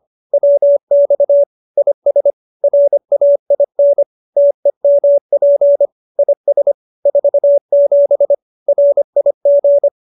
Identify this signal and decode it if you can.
Morse code, heard as a keyed tone, about 25 wpm, 570 Hz. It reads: WX IS RAIN TEMP IS 47 RIG
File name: QSO306 25